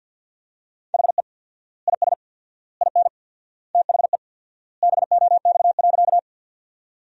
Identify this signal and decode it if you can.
Morse code, heard as a keyed tone, about 50 wpm, 700 Hz. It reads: HE IS IN THE BOX.
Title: 4x Word-Spacing - Sentences from Top 500 Words - 50wpm